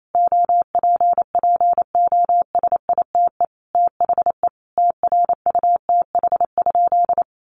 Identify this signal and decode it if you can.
Morse code, heard as a keyed tone, about 28 wpm, 700 Hz. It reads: OPPOSITE THE TRUTH?
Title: Sentences from Top 1300 Words 28wpm